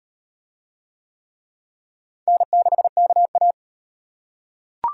Morse code, heard as a keyed tone, about 38 wpm, 700 Hz. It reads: N6KA